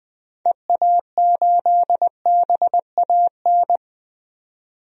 Morse code, heard as a keyed tone, about 20 wpm, 700 Hz. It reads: EA8BAN
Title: DXCC-100 Call Signs - 6 Characters - 20wpm